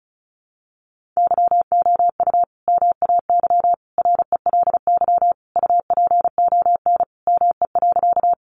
Morse code, heard as a keyed tone, about 34 wpm, 700 Hz. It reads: YOU MAY RELY UPON ME.
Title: Sentences from Top 1200 Words 35wpm